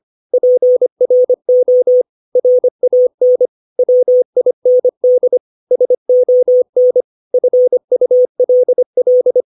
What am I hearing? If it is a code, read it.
Morse code, heard as a keyed tone, about 25 wpm, 500 Hz. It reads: PRO RAN WIND SON FULL
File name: 632 words / 632 Words 25WPM